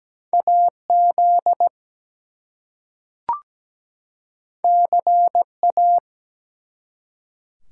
Morse code, heard as a keyed tone, about 17 wpm, 700 Hz. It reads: AZ CA
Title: US States and Canadian Provinces 17wpm